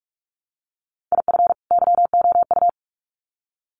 Morse code, heard as a keyed tone, about 44 wpm, 700 Hz. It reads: IF YOU
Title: Sets of 2 Words 45wpm